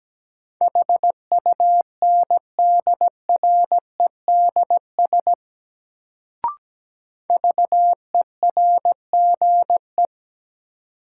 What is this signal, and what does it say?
Morse code, heard as a keyed tone, about 17 wpm, 700 Hz. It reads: HUNDREDS VERGE
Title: Sets of 4 Words - New Word Drill - 17wpm